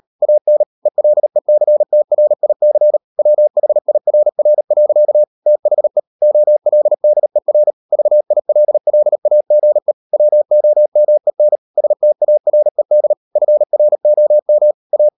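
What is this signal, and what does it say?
Morse code, heard as a keyed tone, about 38 wpm, 600 Hz. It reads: AN ELECTRIC WHIRR. THE OLDER VILLAGE WOMEN STARED FROM A
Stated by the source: Morse code practice at 38 WPM, sending Ken Follett's novel Never in English, from CW med Gnister.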